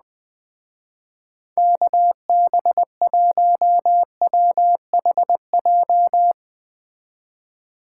Morse code, heard as a keyed tone, about 20 wpm, 700 Hz. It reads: KB1WHJ